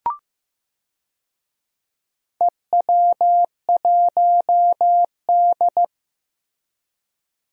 Morse code, heard as a keyed tone, about 15 wpm, 700 Hz. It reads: EW1D
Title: DXCC-100 Call Signs - 4 Characters - Repeated 3x - 15wpm